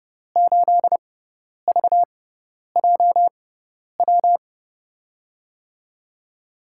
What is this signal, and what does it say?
Morse code, heard as a keyed tone, about 30 wpm, spaced as for 10 wpm, 700 Hz. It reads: 8VJW
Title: USA Call Signs - 6 Characters - 6x Character Spacing - 30wpm